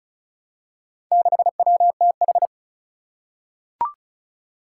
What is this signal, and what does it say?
Morse code, heard as a keyed tone, about 36 wpm, 700 Hz. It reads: 6WTH
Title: Four Challenging Letter-Number Combinations 35wpm